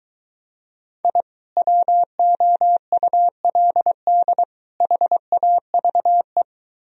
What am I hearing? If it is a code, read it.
Morse code, heard as a keyed tone, about 23 wpm, 700 Hz. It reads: I WOULD HAVE